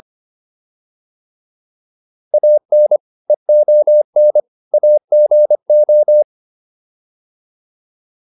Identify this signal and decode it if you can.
Morse code, heard as a keyed tone, about 25 wpm, 600 Hz. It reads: AN EON AGO